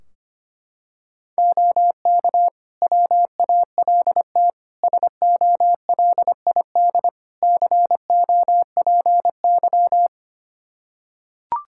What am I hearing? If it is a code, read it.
Morse code, heard as a keyed tone, about 25 wpm, 700 Hz. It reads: OK WALT SOLID COPY
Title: Practice QSO - Exchange 3 - More Info - 25wpm